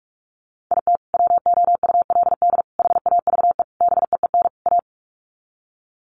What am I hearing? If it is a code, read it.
Morse code, heard as a keyed tone, about 44 wpm, 700 Hz. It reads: IT WOULD HAVE BEEN A
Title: Sets of 5 Words Encore 45wpm